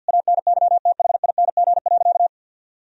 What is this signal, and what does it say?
Morse code, heard as a keyed tone, about 50 wpm, 700 Hz. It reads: ANYTHING.
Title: Sentences from Top 1000 Words - Review Part 3 - 50wpm